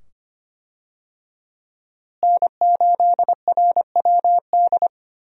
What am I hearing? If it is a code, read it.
Morse code, heard as a keyed tone, about 25 wpm, 700 Hz. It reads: N8RWD
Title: USA Call Signs - 5 Characters - Repeated 2x - 25wpm